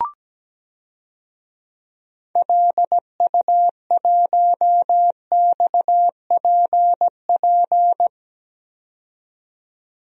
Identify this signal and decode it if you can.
Morse code, heard as a keyed tone, about 17 wpm, 700 Hz. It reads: LU1XPP
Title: DXCC-100 Call Signs - 6 Characters - 17wpm